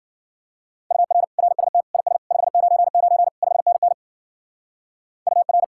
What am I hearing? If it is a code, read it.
Morse code, heard as a keyed tone, about 48 wpm, 700 Hz. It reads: UR RST IS 599 5NN UR
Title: Speed-Racing Common QSO Phrases Encore - 40wpm